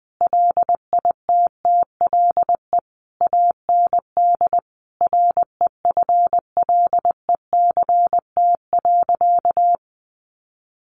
Morse code, heard as a keyed tone, about 20 wpm, 700 Hz. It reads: LITTLE AND REFLECT.
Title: Sentences from Top 1500 Words 20wpm